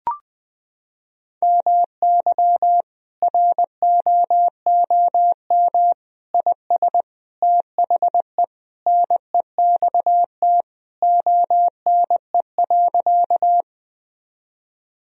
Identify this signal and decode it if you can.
Morse code, heard as a keyed tone, about 21 wpm, 700 Hz. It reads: MY ROOM IS THE NEXT ONE.
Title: Sentences from Top 1000 Words - Review Part 1 - 20wpm